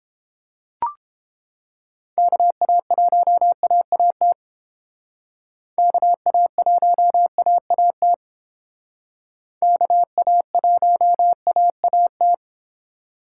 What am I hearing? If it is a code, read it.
Morse code, heard as a keyed tone, about 30 wpm, 700 Hz. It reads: KA1AAT KA1AAT KA1AAT